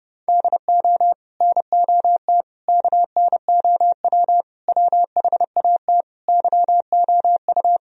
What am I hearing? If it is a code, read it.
Morse code, heard as a keyed tone, about 30 wpm, 700 Hz. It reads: DO NOT KNOW WHAT YOU